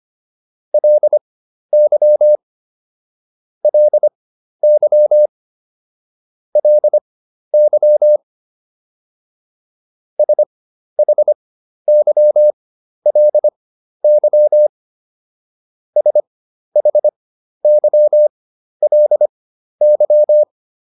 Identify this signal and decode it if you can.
Morse code, heard as a keyed tone, about 25 wpm, spaced as for 12 wpm, 600 Hz. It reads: LY LY LY SHYLY SHYLY